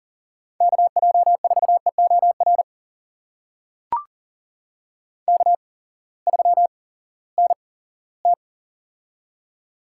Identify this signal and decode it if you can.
Morse code, heard as a keyed tone, about 42 wpm, spaced as for 20 wpm, 700 Hz. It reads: KJ4EOR K3NT
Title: USA Call Signs - 8x Character Spacing - 40wpm